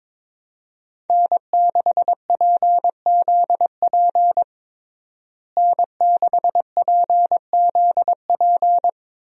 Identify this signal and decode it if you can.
Morse code, heard as a keyed tone, about 22 wpm, 700 Hz. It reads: N6PZP N6PZP
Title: USA Call Signs - 5 Characters - Repeated 2x - 22wpm